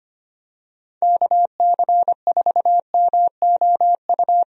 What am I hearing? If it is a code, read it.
Morse code, heard as a keyed tone, about 25 wpm, 700 Hz. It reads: KC4MOU
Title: USA Call Signs - 6 Characters - 25wpm